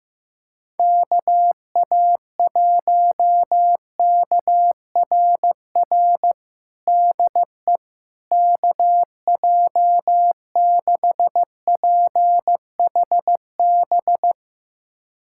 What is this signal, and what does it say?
Morse code, heard as a keyed tone, about 16 wpm, 700 Hz. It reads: KA1KRR DE KJ6PHB
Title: USA Callsign DE Callsign - 6 Characters - 15wpm